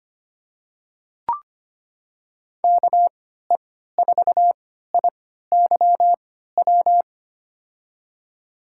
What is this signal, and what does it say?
Morse code, heard as a keyed tone, about 25 wpm, spaced as for 14 wpm, 700 Hz. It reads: KE4IYW
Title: USA Call Signs - 6 Characters - 3x Character Spacing - 25wpm